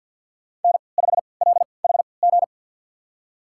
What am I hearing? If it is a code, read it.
Morse code, heard as a keyed tone, about 48 wpm, spaced as for 28 wpm, 700 Hz. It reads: N5LHG
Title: USA Call Signs - 3x Character Spacing - 50wpm